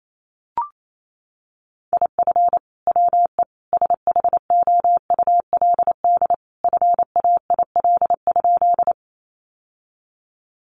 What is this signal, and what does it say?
Morse code, heard as a keyed tone, about 28 wpm, 700 Hz. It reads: IF WE SHOULD FAIL?
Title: Sentences from Top 1100 Words 28wpm